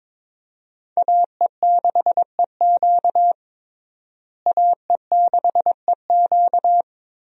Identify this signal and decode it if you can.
Morse code, heard as a keyed tone, about 22 wpm, 700 Hz. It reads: AE6EQ AE6EQ
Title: USA Call Signs - 5 Characters - Repeated 3x - 22wpm